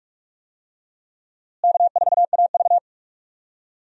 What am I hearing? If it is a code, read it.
Morse code, heard as a keyed tone, about 46 wpm, 700 Hz. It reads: K4AV
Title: USA Call Signs - 4 Characters - Repeated 3x - 45wpm